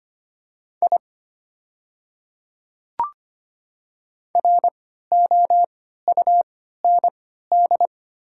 Morse code, heard as a keyed tone, about 25 wpm, spaced as for 14 wpm, 700 Hz. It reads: I ROUND
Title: 3x Character Spacing - Top 200 Words - 25wpm